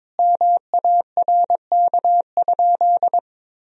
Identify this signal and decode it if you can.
Morse code, heard as a keyed tone, about 22 wpm, 700 Hz. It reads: MARK?
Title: Sentences from Top 1000 Words - Review Part 1 - 22wpm